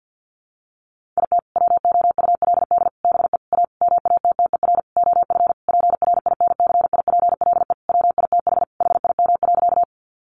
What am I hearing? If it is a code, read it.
Morse code, heard as a keyed tone, about 50 wpm, 700 Hz. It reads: IT WOULD BE A MATTER OF PRINCIPLE WITH HIM.
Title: Sentences from Top 1400 Words 50wpm